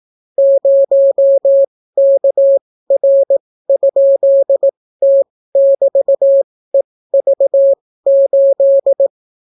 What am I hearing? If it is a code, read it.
Morse code, heard as a keyed tone, about 18 wpm, spaced as for 14 wpm, 550 Hz. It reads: 0KR?T<BT>EV8